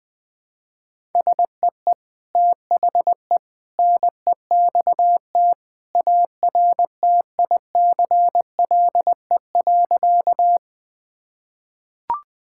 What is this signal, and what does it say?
Morse code, heard as a keyed tone, about 20 wpm, 700 Hz. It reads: SEE THE NEXT ARTICLE.
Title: Sentences from Top 1000 Words - Review Part 5 - 20wpm